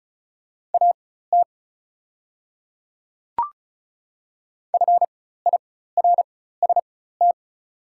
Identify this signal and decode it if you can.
Morse code, heard as a keyed tone, about 35 wpm, spaced as for 16 wpm, 700 Hz. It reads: AT FIRST
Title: 4x Character Spacing - Top 300 Words - 35wpm